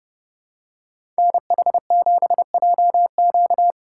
Morse code, heard as a keyed tone, about 30 wpm, 700 Hz. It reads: NH7JQ
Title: USA Call Signs - 5 Characters - 2x Character Spacing - 30wpm